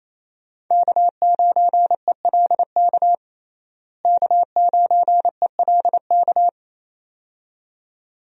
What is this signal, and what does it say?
Morse code, heard as a keyed tone, about 28 wpm, 700 Hz. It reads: K9ELK K9ELK
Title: USA Call Signs - 5 Characters - Repeated 2x - 28wpm